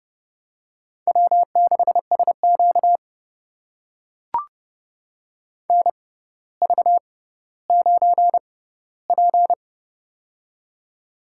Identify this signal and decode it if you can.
Morse code, heard as a keyed tone, about 30 wpm, spaced as for 15 wpm, 700 Hz. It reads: W6SQ NV9P
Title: USA Call Signs - 4 Characters - 6x Character Spacing - 30wpm